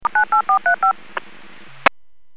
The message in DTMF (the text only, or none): 98435